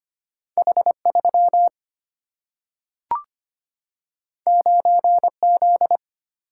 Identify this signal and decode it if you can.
Morse code, heard as a keyed tone, about 25 wpm, 700 Hz. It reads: H3 9Z
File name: Two Letter-Number Combinations - 25wpm